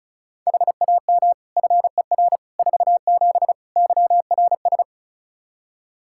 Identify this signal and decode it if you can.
Morse code, heard as a keyed tone, about 34 wpm, 700 Hz. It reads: HAM FER 47 YRS
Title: K1USN SST QSO Practice 35wpm